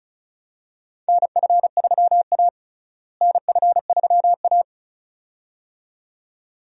Morse code, heard as a keyed tone, about 34 wpm, 700 Hz. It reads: NF3A NF3A
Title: USA Call Signs - 4 Characters - Repeated 2x - 35wpm